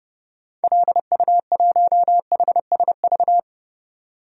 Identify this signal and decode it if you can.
Morse code, heard as a keyed tone, about 31 wpm, 700 Hz. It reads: LU1HSV
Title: DXCC-100 Call Signs - 6 Characters - 30wpm